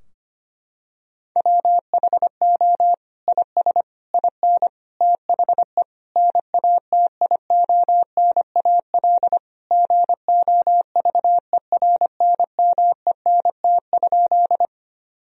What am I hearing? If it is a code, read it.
Morse code, heard as a keyed tone, about 25 wpm, 700 Hz. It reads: WHO IS IN THE NATIONAL GOVERNMENT?
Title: Sentences from Top 1000 Words - Review Part 4 - 25wpm